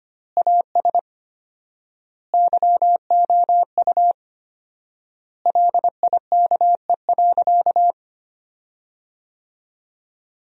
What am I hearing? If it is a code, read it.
Morse code, heard as a keyed tone, about 25 wpm, 700 Hz. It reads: AS YOU LIKE.
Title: 4x Word-Spacing - Sentences from Top 500 Words - 25wpm